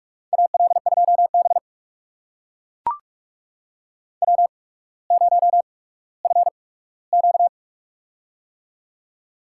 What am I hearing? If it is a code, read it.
Morse code, heard as a keyed tone, about 44 wpm, spaced as for 19 wpm, 700 Hz. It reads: AL2B W0FQ
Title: USA Call Signs - 8x Character Spacing - 45wpm